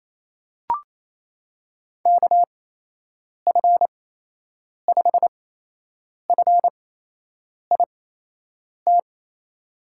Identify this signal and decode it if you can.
Morse code, heard as a keyed tone, about 28 wpm, spaced as for 8 wpm, 700 Hz. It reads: KF5FIT
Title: USA Call Signs - 8x Character Spacing - 28wpm